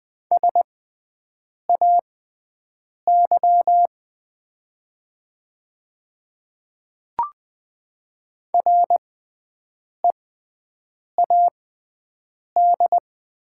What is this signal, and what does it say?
Morse code, heard as a keyed tone, about 21 wpm, spaced as for 7 wpm, 700 Hz. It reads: SAY READ
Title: 6x Character Spacing - Top 200 Words - 20wpm